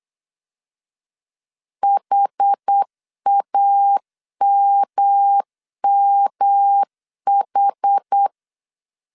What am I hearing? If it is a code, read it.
Morse code, heard as a keyed tone, about 8 wpm, 800 Hz. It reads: HAMMH
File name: CW / DET / 1 3 1500C S B 8wpm 4141